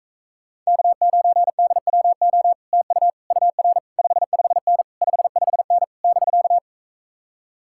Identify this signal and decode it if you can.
Morse code, heard as a keyed tone, about 45 wpm, 700 Hz. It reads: K9DWO TU UR 55N 55N <BK>